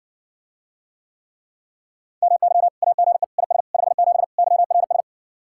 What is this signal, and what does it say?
Morse code, heard as a keyed tone, about 61 wpm, 700 Hz. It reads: MY AGE IS 57 YRS